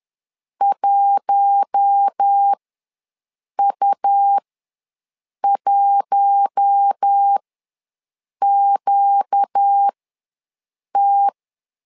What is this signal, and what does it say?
Morse code, heard as a keyed tone, about 10 wpm, spaced as for 6 wpm, 800 Hz. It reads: 1U1QT
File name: LEZIONI / 4 Q1 ws 10wpm 2538